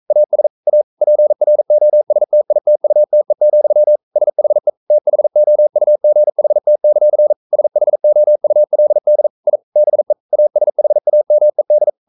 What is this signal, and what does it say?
Morse code, heard as a keyed tone, about 44 wpm, 600 Hz. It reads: AS A PROSTITUTE, SHE THOUGHT; SHOULD I BE ASHAMED
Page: Romanen Never af Ken Follet på engelsk.